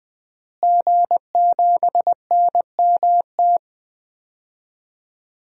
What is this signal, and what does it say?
Morse code, heard as a keyed tone, about 20 wpm, 700 Hz. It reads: G7NMT
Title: DXCC-100 Call Signs - 5 Characters - Repeated 3x - 20wpm